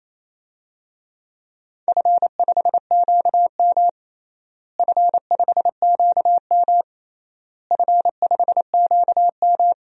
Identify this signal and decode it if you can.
Morse code, heard as a keyed tone, about 28 wpm, 700 Hz. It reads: F5QM F5QM F5QM
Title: DXCC-100 Call Signs - 4 Characters - Repeated 3x - 28wpm